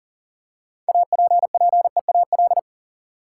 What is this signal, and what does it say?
Morse code, heard as a keyed tone, about 40 wpm, 700 Hz. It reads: APPEAL
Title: Six Letter Words - 40wpm